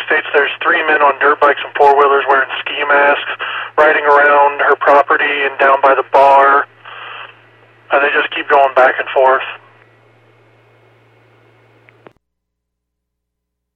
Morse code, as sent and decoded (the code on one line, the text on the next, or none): none